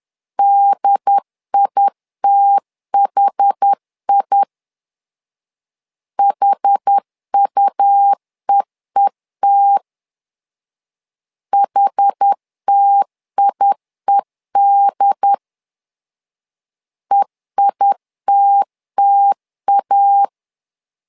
Morse code, heard as a keyed tone, about 10 wpm, 800 Hz. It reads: DITHI HUEET HTIED EITTA
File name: LEZIONI / 2 UHDN / 2 UHD 10wpm 1826